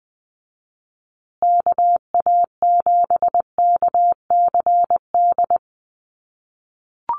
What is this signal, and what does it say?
Morse code, heard as a keyed tone, about 20 wpm, 700 Hz. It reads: KA7KCD
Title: USA Call Signs Repeated 3x 20wpm